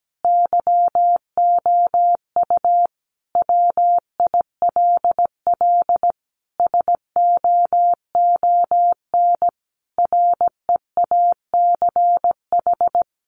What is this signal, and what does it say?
Morse code, heard as a keyed tone, about 17 wpm, 700 Hz. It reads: YOU WILL SOON REACH
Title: Sentences from Top 400 Words 17wpm